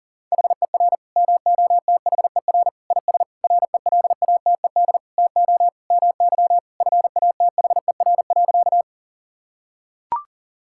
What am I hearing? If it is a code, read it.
Morse code, heard as a keyed tone, about 40 wpm, 700 Hz. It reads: HER MOTHER IS RELATED TO MY FATHER.
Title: Sentences from Top 1000 Words - Review Part 5 - 40wpm